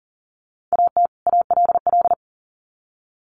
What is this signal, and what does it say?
Morse code, heard as a keyed tone, about 39 wpm, 700 Hz. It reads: AT ALL
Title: Sets of 2 Words 40wpm